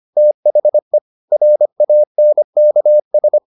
Morse code, heard as a keyed tone, about 25 wpm, 600 Hz. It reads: THE RANKS